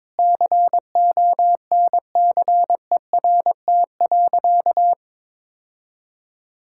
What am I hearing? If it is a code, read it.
Morse code, heard as a keyed tone, about 22 wpm, 700 Hz. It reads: CONCERT.
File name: practice / 22wpm / Sentences with 8-Word Phrases - 22wpm